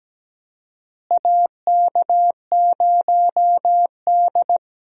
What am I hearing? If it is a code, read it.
Morse code, heard as a keyed tone, about 17 wpm, 700 Hz. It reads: AK0D
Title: USA Call Signs - 4 Characters - Repeated 3x - 17wpm